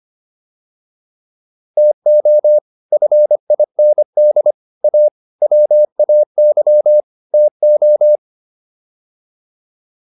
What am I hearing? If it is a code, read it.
Morse code, heard as a keyed tone, about 25 wpm, 600 Hz. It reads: TO FIND A WAY TO